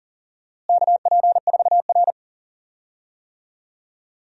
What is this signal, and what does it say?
Morse code, heard as a keyed tone, about 40 wpm, 700 Hz. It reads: KP4R
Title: DXCC-100 Call Signs - 4 Characters - Repeated 3x - 40wpm